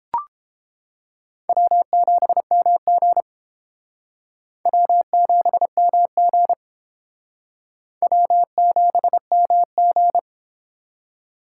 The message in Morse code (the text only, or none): W7MG W7MG W7MG